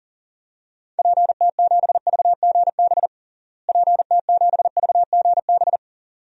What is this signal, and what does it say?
Morse code, heard as a keyed tone, about 40 wpm, 700 Hz. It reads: PT7VGB PT7VGB
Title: DXCC-100 Call Signs - 6 Characters - Repeated 2x - 40wpm